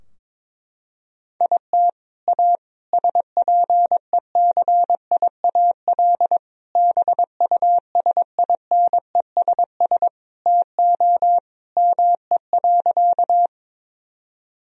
Morse code, heard as a keyed tone, about 22 wpm, 700 Hz. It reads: IT A SPECIAL BUSINESS TO ME.
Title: Sentences from Top 1000 Words - Review Part 3 - 22wpm